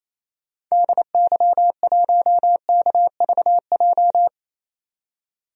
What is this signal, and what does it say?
Morse code, heard as a keyed tone, about 28 wpm, 700 Hz. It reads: DY1KVJ
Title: DXCC-100 Call Signs - 6 Characters - 28wpm